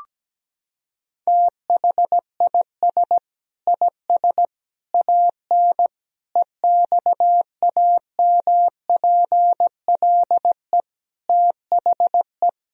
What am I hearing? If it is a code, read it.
Morse code, heard as a keyed tone, about 17 wpm, 700 Hz. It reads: THIS IS AN EXAMPLE THE
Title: Sentences from Top 1000 Words - Review Part 1 - 17wpm